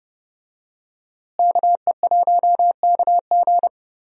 Morse code, heard as a keyed tone, about 30 wpm, 700 Hz. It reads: KE1KG
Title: USA Call Signs - 5 Characters - Repeated 2x - 30wpm